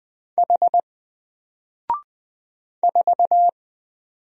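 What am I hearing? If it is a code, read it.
Morse code, heard as a keyed tone, about 20 wpm, 700 Hz. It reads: H 4
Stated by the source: Single letter and number Morse drill at 20 wpm, from N4LCD.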